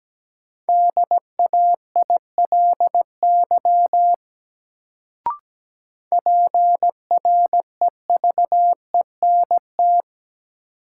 Morse code, heard as a keyed tone, about 17 wpm, 700 Hz. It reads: DAILY PREVENT
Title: Sentences from Top 1100 Words - New Word Drill Alt - 17wpm